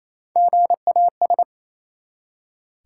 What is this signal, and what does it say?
Morse code, heard as a keyed tone, about 28 wpm, 700 Hz. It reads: GAS
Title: Three Letter Words - 28wpm